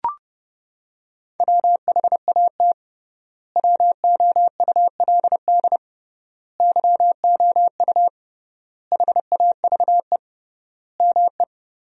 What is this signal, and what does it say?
Morse code, heard as a keyed tone, about 30 wpm, 700 Hz. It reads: WHAT WOULD YOU HAVE ME
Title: 3x Word-Spacing - Sentences from Top 300 Words - 30wpm